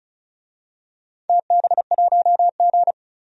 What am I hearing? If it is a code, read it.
Morse code, heard as a keyed tone, about 36 wpm, 700 Hz. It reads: TB1G
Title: DXCC-100 Call Signs - 4 Characters - 35wpm